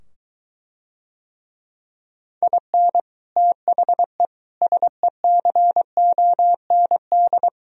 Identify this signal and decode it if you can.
Morse code, heard as a keyed tone, about 23 wpm, 700 Hz. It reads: IN THE SECOND